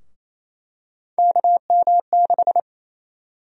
Morse code, heard as a keyed tone, about 28 wpm, 700 Hz. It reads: KM6
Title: USA Call Sign Prefixes - 28wpm